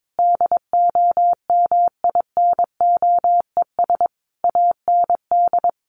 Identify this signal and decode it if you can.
Morse code, heard as a keyed tone, about 22 wpm, 700 Hz. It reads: DOMINOES AND
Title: A Sound Of Thunder 22wpm